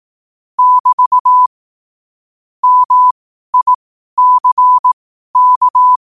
Morse code, heard as a keyed tone, about 18 wpm, spaced as for 12 wpm, 1000 Hz. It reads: <BT> MICK
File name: qotdfiles / 12wpm